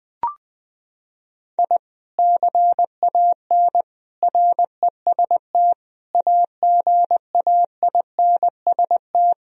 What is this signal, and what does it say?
Morse code, heard as a keyed tone, about 20 wpm, 700 Hz. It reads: I CAN REST AGAINST
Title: Sentences from Top 1000 Words - Review Part 1 - 20wpm